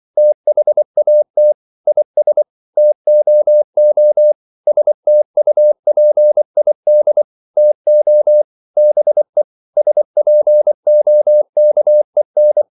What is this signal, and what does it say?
Morse code, heard as a keyed tone, about 24 wpm, 600 Hz. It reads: THAT IS TOO STUPID TO BE SPOKEN
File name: CW / Practice / EnglishQuotations0324